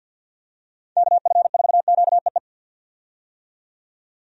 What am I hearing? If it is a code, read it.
Morse code, heard as a keyed tone, about 53 wpm, 700 Hz. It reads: KF4QEE
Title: USA Call Signs - 6 Characters - 50wpm